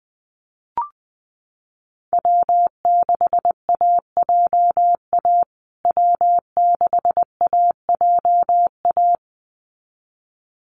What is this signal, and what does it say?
Morse code, heard as a keyed tone, about 20 wpm, 700 Hz. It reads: W6AJA W6AJA
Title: USA Call Signs Repeated 2x 20wpm